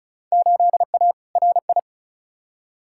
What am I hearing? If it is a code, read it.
Morse code, heard as a keyed tone, about 36 wpm, 700 Hz. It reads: 8A RI
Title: Field Day Exchanges Encore - 35wpm